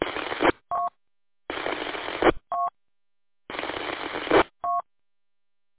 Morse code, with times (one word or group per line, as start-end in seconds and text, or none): none